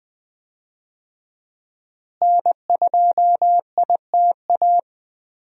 Morse code, heard as a keyed tone, about 20 wpm, 700 Hz. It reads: N2ITA
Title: USA Call Signs - 5 Characters - Repeated 3x - 20wpm